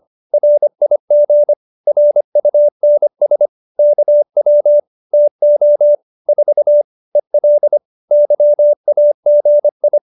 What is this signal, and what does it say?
Morse code, heard as a keyed tone, about 25 wpm, 590 Hz. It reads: RIG RUNS KW TO 4 EL YAGI